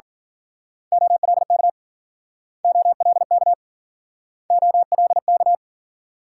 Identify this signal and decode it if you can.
Morse code, heard as a keyed tone, about 47 wpm, 700 Hz. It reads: OLK OLK OLK
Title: Speed-Racing Multi-letter Phonemes - 40wpm